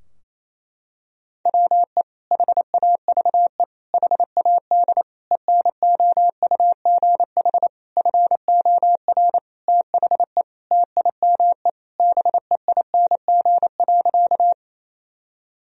Morse code, heard as a keyed tone, about 28 wpm, 700 Hz. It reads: WE HAVE HAD ENOUGH FOR THE TIME BEING.
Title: Sentences from Top 1000 Words - Review Part 3 - 28wpm